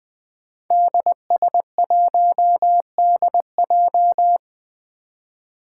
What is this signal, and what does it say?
Morse code, heard as a keyed tone, about 20 wpm, 700 Hz. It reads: DS1DJ
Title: Five Challenging Letter-Number Combinations 20wpm